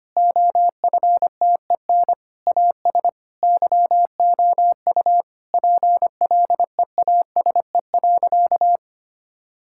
Morse code, heard as a keyed tone, about 25 wpm, 700 Hz. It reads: OFTEN AS YOU PLEASE.
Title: Sentences from Top 1600 Words 25wpm